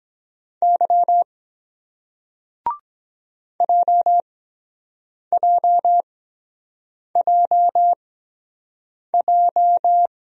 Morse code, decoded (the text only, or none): Y JJJJ